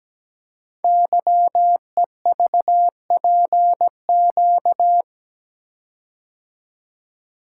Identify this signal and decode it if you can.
Morse code, heard as a keyed tone, about 17 wpm, 700 Hz. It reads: YEVPQ
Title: Code Groups - 5 Letters - 17wpm